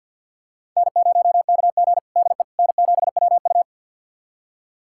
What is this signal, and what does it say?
Morse code, heard as a keyed tone, about 50 wpm, 700 Hz. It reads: N0KG DE N7WU
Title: USA Callsign DE Callsign - 4 Characters - 50wpm